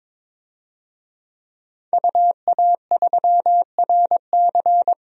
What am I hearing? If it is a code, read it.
Morse code, heard as a keyed tone, about 22 wpm, 700 Hz. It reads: UA3RC
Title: DXCC-100 Call Signs - 5 Characters - 22wpm